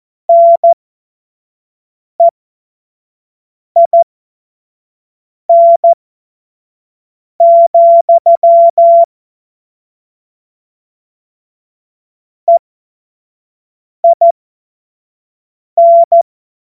Morse code, heard as a keyed tone, about 14 wpm, spaced as for 5 wpm, 680 Hz. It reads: NEIN, EIN
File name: Morsekurs / Witze / Witz 04. April 2023 25 F70